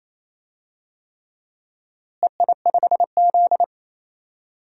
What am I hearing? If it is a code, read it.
Morse code, heard as a keyed tone, about 28 wpm, 700 Hz. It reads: EI5Z
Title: DXCC-100 Call Signs - 4 Characters - Repeated 3x - 28wpm